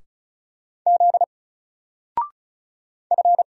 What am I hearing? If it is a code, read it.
Morse code, heard as a keyed tone, about 36 wpm, 700 Hz. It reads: Z F
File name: Single-Letter-Number-35wpm